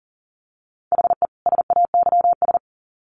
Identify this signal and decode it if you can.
Morse code, heard as a keyed tone, about 40 wpm, 700 Hz. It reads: HE SAYS
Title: Sets of 2 Words 40wpm